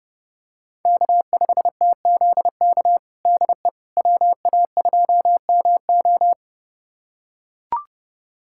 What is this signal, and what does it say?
Morse code, heard as a keyed tone, about 32 wpm, 700 Hz. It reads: K5TZK DE WA2MO
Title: USA Callsign DE Callsign - 5 Characters - 30wpm